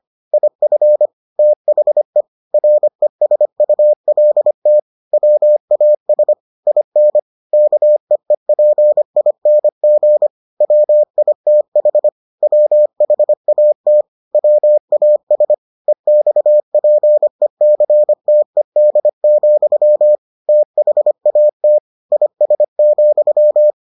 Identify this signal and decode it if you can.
Morse code, heard as a keyed tone, about 25 wpm, 600 Hz. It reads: IF THE RESULT WAS IN KEEPING WITH WHAT WAS EXPECTED, THAT IS,